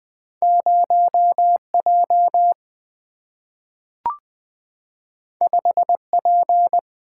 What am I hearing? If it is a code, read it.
Morse code, heard as a keyed tone, about 20 wpm, 700 Hz. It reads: 0J 5P
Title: Single-Letter Single-Number 20wpm